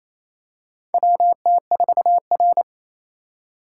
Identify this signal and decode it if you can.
Morse code, heard as a keyed tone, about 29 wpm, 700 Hz. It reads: WT4R
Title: USA Call Signs - 4 Characters - 3x Character Spacing - 28wpm